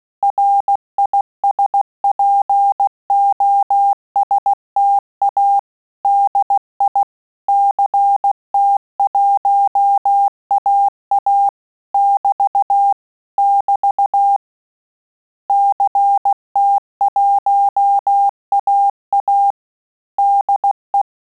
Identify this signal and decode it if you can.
Morse code, heard as a keyed tone, about 16 wpm, 800 Hz. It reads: RISPOSTA DI CT1AA <BT> <BT> CT1AA DE
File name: Telegraphy-group / QSO CW 15 PPM